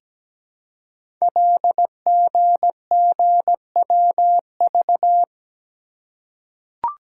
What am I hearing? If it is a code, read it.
Morse code, heard as a keyed tone, about 17 wpm, 700 Hz. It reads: LGGWV
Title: Code Groups - 5 Letters - 17wpm